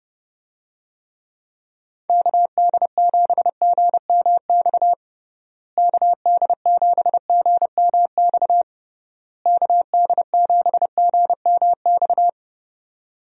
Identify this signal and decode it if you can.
Morse code, heard as a keyed tone, about 30 wpm, 700 Hz. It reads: KD7GMX KD7GMX KD7GMX
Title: USA Call Signs - 6 Characters - Repeated 3x - 30wpm